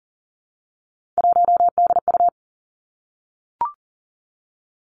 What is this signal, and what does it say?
Morse code, heard as a keyed tone, about 37 wpm, 700 Hz. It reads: 1DU
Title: Three Challenging Letter-Number Combinations 40wpm